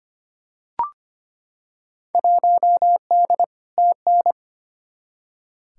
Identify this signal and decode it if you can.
Morse code, heard as a keyed tone, about 25 wpm, 700 Hz. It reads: 1D TN
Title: Field Day Exchanges - 25wpm